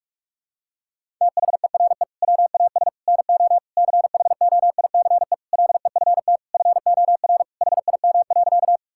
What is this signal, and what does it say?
Morse code, heard as a keyed tone, about 42 wpm, 700 Hz. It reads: THERE WAS NO CHOICE LEFT FOR HIM.
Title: Sentences from Top 1000 Words - Review Part 6 - 45wpm